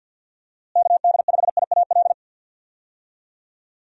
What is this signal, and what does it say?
Morse code, heard as a keyed tone, about 47 wpm, 700 Hz. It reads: KD5IAL
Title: USA Call Signs - 6 Characters - Repeated 2x - 50wpm